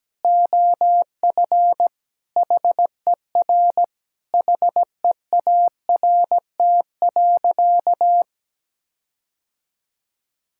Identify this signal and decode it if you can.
Morse code, heard as a keyed tone, about 17 wpm, 700 Hz. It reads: OF HER HEART.
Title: Sentences from Top 1600 Words 17wpm